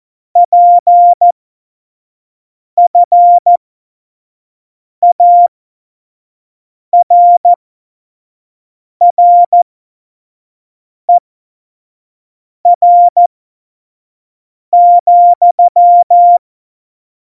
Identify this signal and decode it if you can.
Morse code, heard as a keyed tone, about 14 wpm, spaced as for 5 wpm, 700 Hz. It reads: PFARRER,
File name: Morsekurs / Witze / Witz 29. Dezember 2020, 25 F70